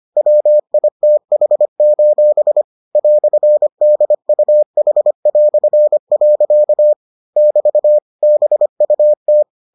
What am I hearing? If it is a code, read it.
Morse code, heard as a keyed tone, about 25 wpm, 600 Hz. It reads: WITH: "DUH". <BT> BUT